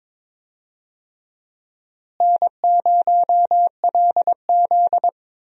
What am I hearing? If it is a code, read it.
Morse code, heard as a keyed tone, about 22 wpm, 700 Hz. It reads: N0LZ
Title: USA Call Signs - 4 Characters - Repeated 2x - 22wpm